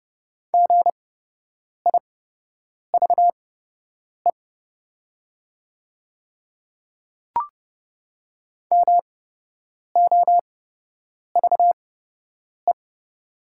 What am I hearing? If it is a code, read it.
Morse code, heard as a keyed tone, about 30 wpm, spaced as for 8 wpm, 700 Hz. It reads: GIVE MOVE
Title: 8x Character Spacing - Top 200 Words - 30wpm